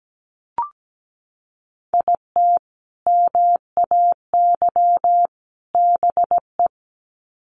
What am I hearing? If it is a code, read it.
Morse code, heard as a keyed tone, about 17 wpm, 700 Hz. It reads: IT MAY BE
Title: Sets of 3 Words 17wpm